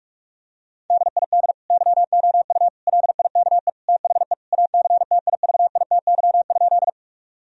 Morse code, heard as a keyed tone, about 41 wpm, 700 Hz. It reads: DID YOU LIKE THE ACTIVITY?
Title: Sentences from Top 1000 Words - Review Part 4 - 45wpm